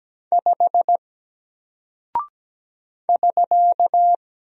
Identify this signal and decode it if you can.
Morse code, heard as a keyed tone, about 17 wpm, 700 Hz. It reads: 5 <SK>
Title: Single Character (Letter:Number:Punctuation:Prosign) Alt 17wpm